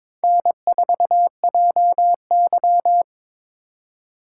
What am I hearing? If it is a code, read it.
Morse code, heard as a keyed tone, about 27 wpm, 700 Hz. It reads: N4JY
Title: Speed Racing - USA Call Signs 22wpm